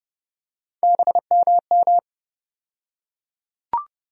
Morse code, heard as a keyed tone, about 29 wpm, 700 Hz. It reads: BMM
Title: USA Call Sign Postfixes - 3 Letters 30wpm